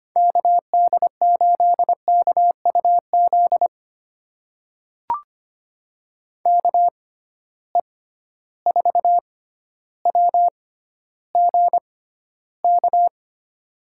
Morse code, heard as a keyed tone, about 25 wpm, spaced as for 13 wpm, 700 Hz. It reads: KD8KUZ KE4WGK
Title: USA Call Signs - 6 Characters - 6x Character Spacing - 25wpm